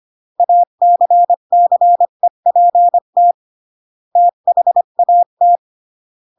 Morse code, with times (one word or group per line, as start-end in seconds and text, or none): 0.4-3.3 ACCEPT
4.2-5.6 THAT